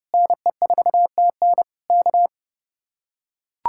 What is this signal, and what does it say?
Morse code, heard as a keyed tone, about 32 wpm, 700 Hz. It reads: NE4TN K